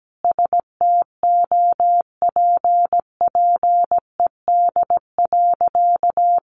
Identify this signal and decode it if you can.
Morse code, heard as a keyed tone, about 17 wpm, 700 Hz. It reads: STOPPED.
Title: A Sound Of Thunder 17wpm